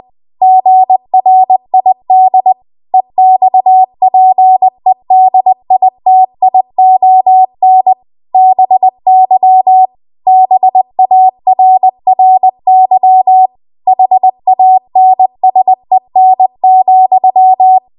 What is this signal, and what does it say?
Morse code, heard as a keyed tone, about 20 wpm, 750 Hz. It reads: GRID EXPEDITION BY BARRY HANSEN,
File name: Morse / 20WPM